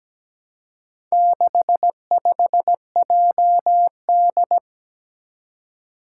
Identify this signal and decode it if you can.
Morse code, heard as a keyed tone, about 17 wpm, 700 Hz. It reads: 65JD
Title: Four Challenging Letter-Number Combinations 17wpm